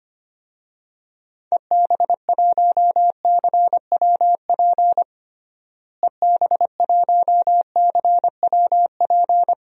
Morse code, heard as a keyed tone, about 25 wpm, 700 Hz. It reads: EB1CWP EB1CWP
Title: DXCC-100 Call Signs - 6 Characters - Repeated 3x - 25wpm